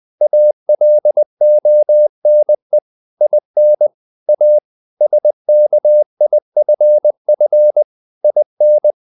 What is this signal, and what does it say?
Morse code, heard as a keyed tone, about 20 wpm, 600 Hz. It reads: ALONE IN A SKIFF IN
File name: the old man and the sea / the old man and the sea part 1 20wpm